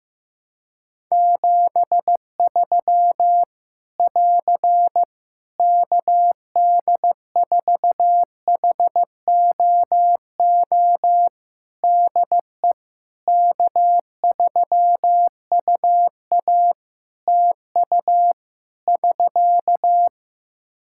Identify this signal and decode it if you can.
Morse code, heard as a keyed tone, about 15 wpm, 700 Hz. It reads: 73 <AR> KD4HOO DE K3UA TU <SK>
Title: Practice QSO - Exchange 4 - Ending - 15wpm